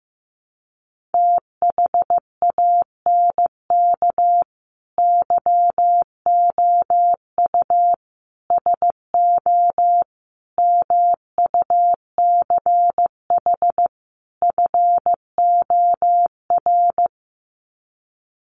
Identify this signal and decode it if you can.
Morse code, heard as a keyed tone, about 15 wpm, 700 Hz. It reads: THANK YOU SO MUCH FOR
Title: Sets of 5 Words 15wpm